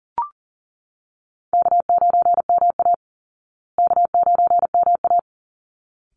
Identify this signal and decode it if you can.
Morse code, heard as a keyed tone, about 39 wpm, 700 Hz. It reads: K9MA K9MA
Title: Call Signs Repeated 3x - 40wpm